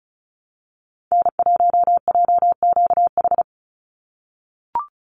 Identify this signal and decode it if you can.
Morse code, heard as a keyed tone, about 34 wpm, 700 Hz. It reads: N1JQH
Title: USA Call Signs 35wpm